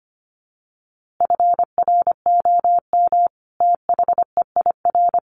Morse code, heard as a keyed tone, about 25 wpm, 700 Hz. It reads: FROM THEIR
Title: A Sound Of Thunder 25wpm